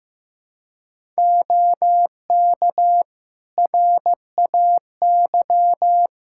Morse code, heard as a keyed tone, about 15 wpm, 700 Hz. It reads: OK RAY
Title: Practice QSO - Exchange 3 - More Info - 15wpm